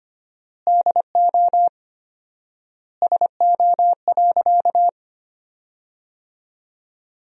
Morse code, heard as a keyed tone, about 25 wpm, 700 Hz. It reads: DO SO.
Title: 4x Word-Spacing - Sentences from Top 500 Words - 25wpm